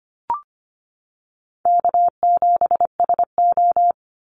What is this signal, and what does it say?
Morse code, heard as a keyed tone, about 25 wpm, 700 Hz. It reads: K7SO
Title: NA SOTA Activator Callsigns 25wpm